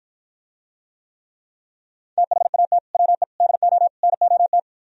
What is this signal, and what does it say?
Morse code, heard as a keyed tone, about 53 wpm, 700 Hz. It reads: THAT WE DO NOT